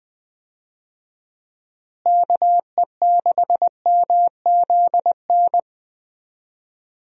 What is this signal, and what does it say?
Morse code, heard as a keyed tone, about 20 wpm, 700 Hz. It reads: KE6MZN
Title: USA Call Signs - 6 Characters - Repeated 3x - 20wpm